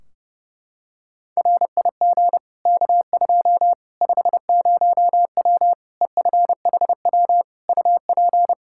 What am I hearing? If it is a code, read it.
Morse code, heard as a keyed tone, about 32 wpm, 700 Hz. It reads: RIG K2 50W EFHW UP
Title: Practice QSO - Exchange 2 - Info - 30wpm